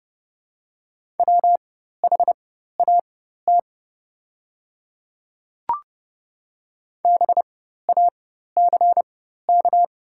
Morse code, heard as a keyed tone, about 30 wpm, spaced as for 14 wpm, 700 Hz. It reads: WHAT BACK
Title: 4x Character Spacing - Top 200 Words - 30wpm